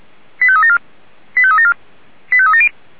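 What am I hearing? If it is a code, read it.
Five-tone selective call (ZVEI), heard as quick runs of five tones, tones of about 70 ms. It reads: 85374 85374 85370
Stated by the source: Five-tone ZVEI selective call received by SDR.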